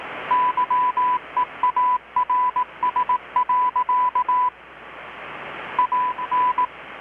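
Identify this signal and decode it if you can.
Morse code, heard as a keyed tone, about 18 wpm, 1000 Hz. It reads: YEARS. AN
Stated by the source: Morse from KPH's Night of Nights XX maritime radiotelegraphy event, received in Chiapas, Mexico about 3,800 km away on a Kenwood R-600 with a grounded 10 m long-wire antenna.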